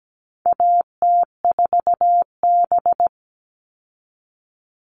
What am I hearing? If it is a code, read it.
Morse code, heard as a keyed tone, about 17 wpm, 700 Hz. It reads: AT4B
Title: All Cut Numbers - 4-Digit 17wpm